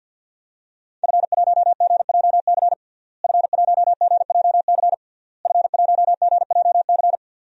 Morse code, heard as a keyed tone, about 50 wpm, 700 Hz. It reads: F1GJC F1GJC F1GJC
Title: DXCC-100 Call Signs - 5 Characters - Repeated 3x - 50wpm